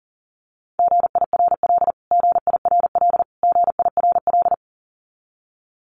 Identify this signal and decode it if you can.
Morse code, heard as a keyed tone, about 40 wpm, 700 Hz. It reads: GIRL GIRL GIRL
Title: Top 500 Words Repeated 40wpm